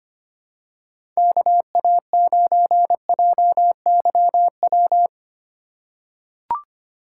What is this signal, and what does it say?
Morse code, heard as a keyed tone, about 25 wpm, 700 Hz. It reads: KA9JYW
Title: USA Call Signs - 4x Character Spacing - 25wpm